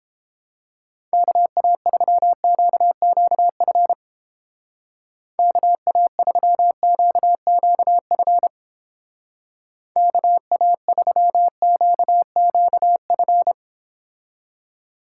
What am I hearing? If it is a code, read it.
Morse code, heard as a keyed tone, about 29 wpm, 700 Hz. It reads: KA3QQF KA3QQF KA3QQF